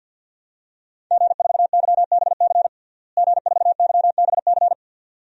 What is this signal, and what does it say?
Morse code, heard as a keyed tone, about 52 wpm, 700 Hz. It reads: G4YBC G4YBC
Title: DXCC-100 Call Signs - 5 Characters - Repeated 2x - 50wpm